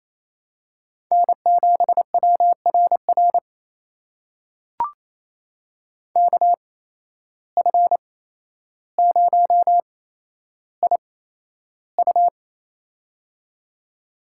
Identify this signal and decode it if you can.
Morse code, heard as a keyed tone, about 28 wpm, spaced as for 12 wpm, 700 Hz. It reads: N7WRR KF0IU
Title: USA Call Signs - 8x Character Spacing - 28wpm